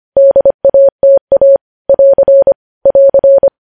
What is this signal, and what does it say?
Morse code, heard as a keyed tone, about 25 wpm, 550 Hz. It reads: DATA <AR> <AR>
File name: podcast / News headlines-1x-25x25